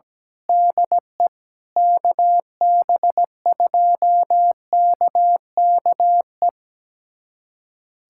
Morse code, heard as a keyed tone, about 17 wpm, 700 Hz. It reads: DE KB2KKE
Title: USA Callsign DE Callsign - 6 Characters - 17wpm